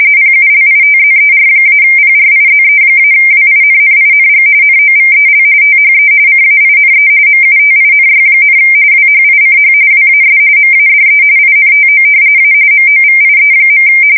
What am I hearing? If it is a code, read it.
RTTY, about 45.45 baud. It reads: PULITZER-PRIZE-WINNING NOVEL THE CAINE MUTINY, WHICH WAS MADE INTO A MOVIE.
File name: fsk170 / battleshipnj